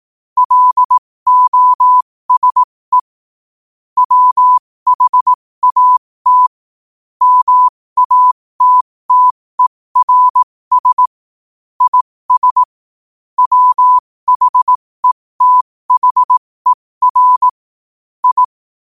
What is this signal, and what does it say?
Morse code, heard as a keyed tone, about 18 wpm, spaced as for 15 wpm, 1000 Hz. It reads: LOSE WHAT MATTERS IS WHETHER I
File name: qotdfiles / 15wpm